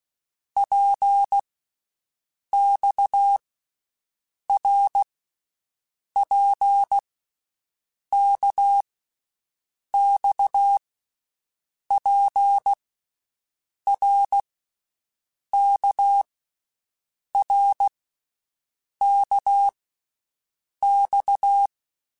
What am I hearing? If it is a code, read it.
Morse code, heard as a keyed tone, about 16 wpm, 800 Hz. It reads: P X R P K X P R K R K X
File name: Basic Letter Groups By7Groups / RKPX / RKPX@10